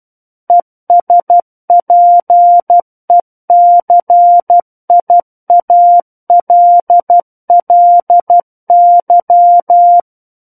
Morse code, heard as a keyed tone, about 12 wpm, 700 Hz. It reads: ESPECIALLY
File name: Book One / 03-Chapter 3 - 12 WPM